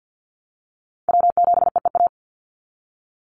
Morse code, heard as a keyed tone, about 50 wpm, 700 Hz. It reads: W7EEA
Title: USA Call Signs 50wpm